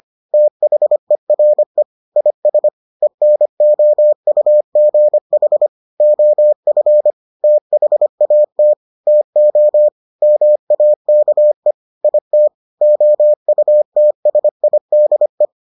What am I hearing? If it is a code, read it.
Morse code, heard as a keyed tone, about 25 wpm, 600 Hz. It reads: THERE IS ENOUGH OF THAT TO MAKE IT OUTSIDE